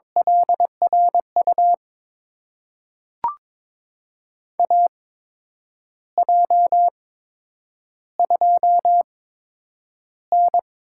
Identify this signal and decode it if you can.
Morse code, heard as a keyed tone, about 22 wpm, spaced as for 8 wpm, 700 Hz. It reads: LRU AJ2N
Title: USA Call Signs - 8x Character Spacing - 22wpm